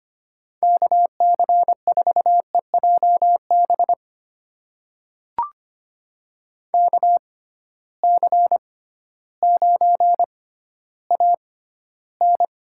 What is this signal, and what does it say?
Morse code, heard as a keyed tone, about 25 wpm, spaced as for 14 wpm, 700 Hz. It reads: KC4EJB KC9AN
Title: USA Call Signs - 6x Character Spacing - 25wpm